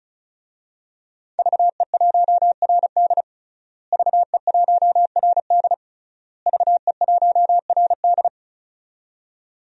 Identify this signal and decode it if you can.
Morse code, heard as a keyed tone, about 36 wpm, 700 Hz. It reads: VE1RD VE1RD VE1RD
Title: DXCC-100 Call Signs - 5 Characters - Repeated 3x - 35wpm